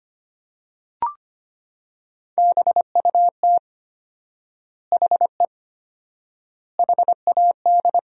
Morse code, heard as a keyed tone, about 25 wpm, 700 Hz. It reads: BUT HE HAD
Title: 4x Word-Spacing - Sentences from Top 100 Words - 25wpm